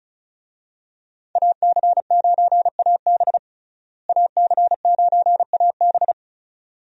Morse code, heard as a keyed tone, about 36 wpm, 700 Hz. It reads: AC9AB AC9AB
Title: USA Call Signs - 5 Characters - Repeated 2x - 35wpm